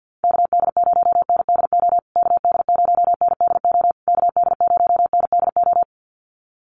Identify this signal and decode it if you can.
Morse code, heard as a keyed tone, about 49 wpm, 700 Hz. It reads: KD0NDO KD0NDO KD0NDO
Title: USA Call Signs Repeated 3x 50wpm